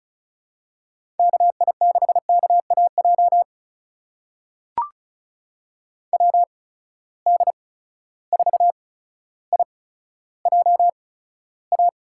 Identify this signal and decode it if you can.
Morse code, heard as a keyed tone, about 36 wpm, spaced as for 15 wpm, 700 Hz. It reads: KI6KAJ WD4IJA
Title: USA Call Signs - 6 Characters - 8x Character Spacing - 35wpm